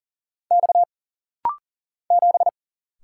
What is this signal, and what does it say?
Morse code, heard as a keyed tone, about 41 wpm, 700 Hz. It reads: X 7
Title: Single Letter-Number Alt - Rapid-Fire 40wpm